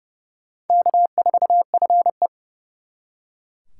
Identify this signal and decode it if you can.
Morse code, heard as a keyed tone, about 38 wpm, 700 Hz. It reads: K4FE